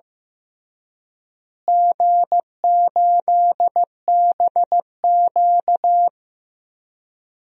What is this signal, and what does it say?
Morse code, heard as a keyed tone, about 15 wpm, 700 Hz. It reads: G8BQ
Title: DXCC-100 Call Signs - 4 Characters - Repeated 3x - 15wpm